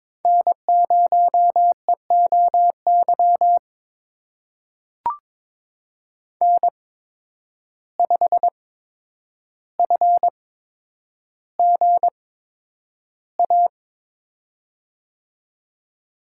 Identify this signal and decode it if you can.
Morse code, heard as a keyed tone, about 22 wpm, spaced as for 9 wpm, 700 Hz. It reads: N0EOY N5FGA
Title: USA Call Signs - 5 Characters - 8x Character Spacing - 22wpm